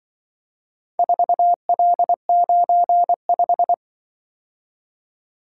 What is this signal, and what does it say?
Morse code, heard as a keyed tone, about 25 wpm, 700 Hz. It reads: 4L95
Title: Speed-Racing - Four Letter-Number Combinations - 20wpm